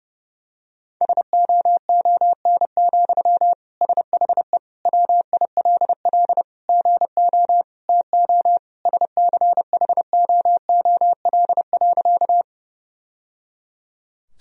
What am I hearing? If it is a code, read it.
Morse code, heard as a keyed tone, about 30 wpm, 700 Hz. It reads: SOON, SHE WILL GO TO SCHOOL.
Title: Sentences from Top 1000 Words - Review Part 1 - 30wpm